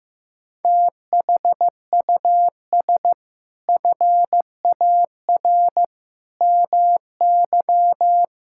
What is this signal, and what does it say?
Morse code, heard as a keyed tone, about 15 wpm, 700 Hz. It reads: THUS FAR MY
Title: Sentences from Top 1000 Words - Review Part 5 - 15wpm